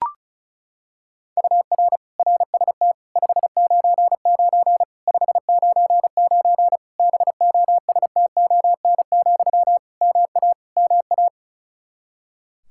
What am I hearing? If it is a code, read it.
Morse code, heard as a keyed tone, about 36 wpm, 700 Hz. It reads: UR RST 599 599 BOSTON, MA MA
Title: Practice QSO - Exchange 1 - Beginning - 35wpm